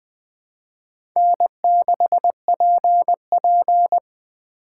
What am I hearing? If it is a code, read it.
Morse code, heard as a keyed tone, about 20 wpm, 700 Hz. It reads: N6PP
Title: USA Call Signs - 4 Characters - 3x Character Spacing - 20wpm